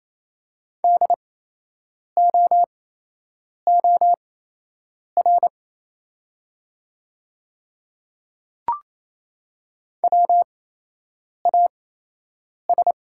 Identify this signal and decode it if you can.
Morse code, heard as a keyed tone, about 28 wpm, spaced as for 8 wpm, 700 Hz. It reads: DOOR WAS
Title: 8x Character Spacing - Top 500 Words - 28wpm